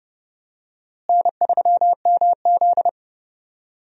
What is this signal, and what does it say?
Morse code, heard as a keyed tone, about 30 wpm, 700 Hz. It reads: N3MZ
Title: USA Call Signs - 4 Characters - Repeated 2x - 30wpm